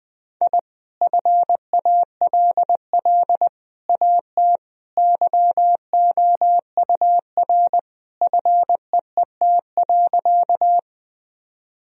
Morse code, heard as a keyed tone, about 20 wpm, 700 Hz. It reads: I FALL AT YOUR FEET.
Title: Sentences from Top 1000 Words - Review Part 2 - 20wpm